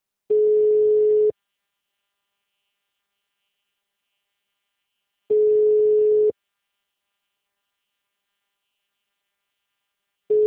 Ringback tone (Europe, 425 Hz).
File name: out- -2028